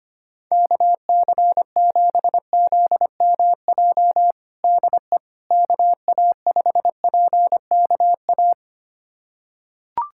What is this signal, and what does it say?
Morse code, heard as a keyed tone, about 25 wpm, 700 Hz. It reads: KC7ZMJ DE KA5PKA
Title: USA Callsign DE Callsign - 6 Characters - 25wpm